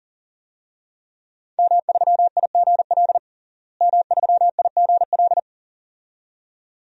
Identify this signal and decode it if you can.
Morse code, heard as a keyed tone, about 37 wpm, 700 Hz. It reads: M3IGL M3IGL
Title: DXCC-100 Call Signs - 5 Characters - Repeated 2x - 40wpm